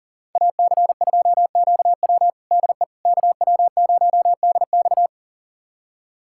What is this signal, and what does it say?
Morse code, heard as a keyed tone, about 40 wpm, 700 Hz. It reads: AC2QW DE KW0DX
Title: USA Callsign DE Callsign - 5 Characters - 40wpm